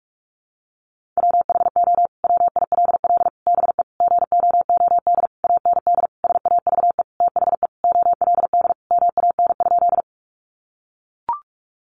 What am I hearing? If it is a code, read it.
Morse code, heard as a keyed tone, about 44 wpm, 700 Hz. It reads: WHO WILL BE GOOD AND SAVE THE OLD MAN?
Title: Sentences from Top 1100 Words 45wpm